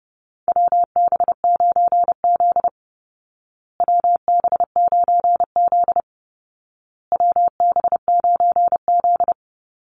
Morse code, heard as a keyed tone, about 30 wpm, 700 Hz. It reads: WB9Z WB9Z WB9Z
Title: Call Signs Repeated 2x - 30wpm